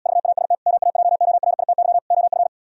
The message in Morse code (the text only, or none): VAST MAJORITY OF